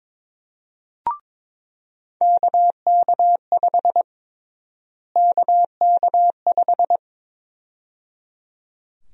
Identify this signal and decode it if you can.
Morse code, heard as a keyed tone, about 21 wpm, 700 Hz. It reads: KK5 KK5
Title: USA Call Sign Prefixes - Repeated 2x - 22wpm